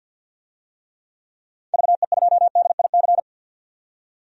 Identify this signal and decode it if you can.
Morse code, heard as a keyed tone, about 47 wpm, 700 Hz. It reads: VE2DIC
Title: DXCC-100 Call Signs - 6 Characters - 50wpm